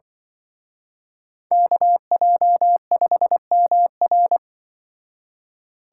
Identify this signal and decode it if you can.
Morse code, heard as a keyed tone, about 24 wpm, 700 Hz. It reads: KJ5MR
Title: Speed Racing - USA 2x2 Call Signs 20wpm